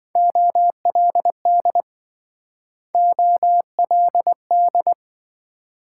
Morse code, heard as a keyed tone, about 27 wpm, 700 Hz. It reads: OLD OLD